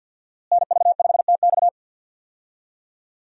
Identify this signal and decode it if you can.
Morse code, heard as a keyed tone, about 46 wpm, 700 Hz. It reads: NV5TX
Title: USA Call Signs - 5 Characters - Repeated 2x - 50wpm